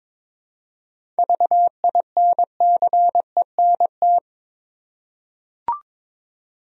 Morse code, heard as a keyed tone, about 22 wpm, 700 Hz. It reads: VINCENT
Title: USA Common Names - 22wpm